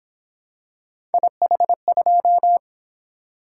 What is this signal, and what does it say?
Morse code, heard as a keyed tone, about 26 wpm, 700 Hz. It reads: IH2
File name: Speed-Racing - Three Letter-Number Combinations - 22wpm